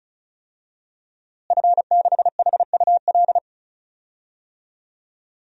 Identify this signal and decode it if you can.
Morse code, heard as a keyed tone, about 36 wpm, 700 Hz. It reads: F6HUL
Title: DXCC-100 Call Signs - 5 Characters - Repeated 2x - 35wpm